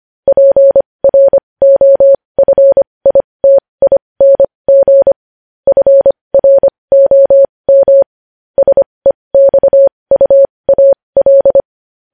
Morse code, heard as a keyed tone, about 25 wpm, spaced as for 20 wpm, 550 Hz. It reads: PROFITING FROM SEXUAL
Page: news-headlines-in-morse-code-at-20-wpm